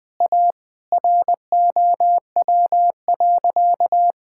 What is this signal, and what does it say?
Morse code, heard as a keyed tone, about 20 wpm, 700 Hz. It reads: A ROW.
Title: Sentences with 7-Word Phrases - 20wpm